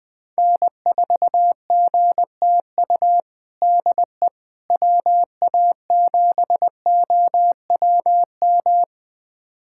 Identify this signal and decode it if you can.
Morse code, heard as a keyed tone, about 20 wpm, 700 Hz. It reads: N4GTU DE WA7OWM
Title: USA Callsign DE Callsign 20wpm